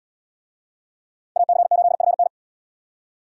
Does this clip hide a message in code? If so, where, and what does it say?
Yes: Morse code from 1.4 to 2.3 s, heard as a keyed tone, about 70 wpm, 690 Hz. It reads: AC8FN